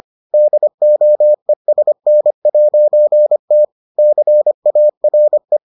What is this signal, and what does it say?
Morse code, heard as a keyed tone, about 25 wpm, 600 Hz. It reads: DOESN'T CARE